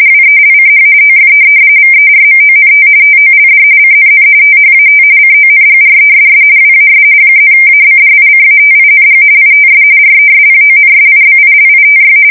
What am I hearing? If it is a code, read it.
RTTY, about 45.45 baud. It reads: WELCH OF VERMONT TOLD CNN THAT CONGRESS WAS +ABDICATING ITS RESPONSIBILITY
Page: 170 Shift 2125/2295